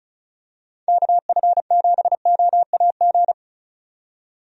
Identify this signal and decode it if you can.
Morse code, heard as a keyed tone, about 34 wpm, 700 Hz. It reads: KF7OAG
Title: USA Call Signs - 6 Characters - 35wpm